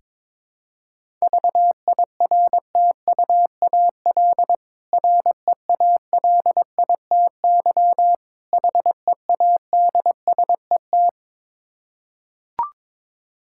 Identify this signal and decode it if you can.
Morse code, heard as a keyed tone, about 22 wpm, 690 Hz. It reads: VIRTUAL REALITY HEADSET
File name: Difficult Words with Hints - 22wpm